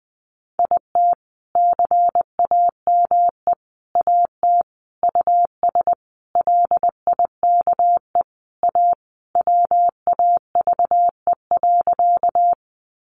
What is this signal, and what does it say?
Morse code, heard as a keyed tone, about 20 wpm, 700 Hz. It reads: IT CAME AT US LIKE A WAVE.
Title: Sentences from Top 1400 Words 20wpm